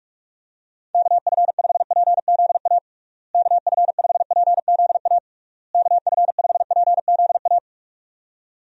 Morse code, heard as a keyed tone, about 46 wpm, 700 Hz. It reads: KF5PZA KF5PZA KF5PZA
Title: USA Call Signs - 6 Characters - Repeated 3x - 45wpm